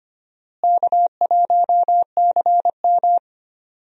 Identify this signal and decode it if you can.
Morse code, heard as a keyed tone, about 25 wpm, 700 Hz. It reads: K1CM
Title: USA Call Signs - 4 Characters - Repeated 2x - 25wpm